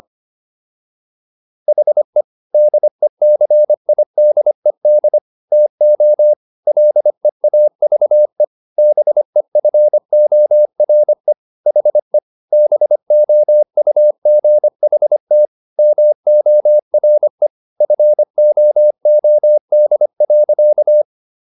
Morse code, heard as a keyed tone, about 25 wpm, 600 Hz. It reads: HE DECIDED TO LEAVE BEFORE HE BOUGHT MORE FOOD.